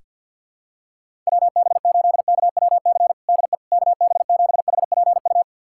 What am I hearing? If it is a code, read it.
Morse code, heard as a keyed tone, about 52 wpm, 700 Hz. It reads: WB8KWC DE KB7HPU
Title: USA Callsign DE Callsign - 6 Characters - 50wpm